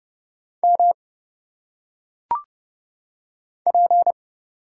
Morse code, heard as a keyed tone, about 29 wpm, 700 Hz. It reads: M P
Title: Single Letters - Phonetic Pronunciation 30wpm